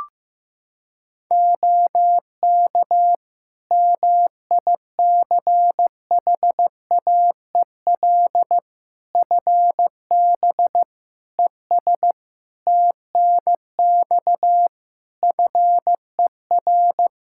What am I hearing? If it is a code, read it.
Morse code, heard as a keyed tone, about 16 wpm, 700 Hz. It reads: OK MICHAEL FB ES TNX FER
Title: K1USN SST QSO Practice 15wpm